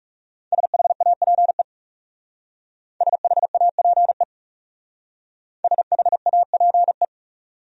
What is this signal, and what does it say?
Morse code, heard as a keyed tone, about 37 wpm, 700 Hz. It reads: SHAPE SHAPE SHAPE
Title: Speed-Racing Top 500 Words 30wpm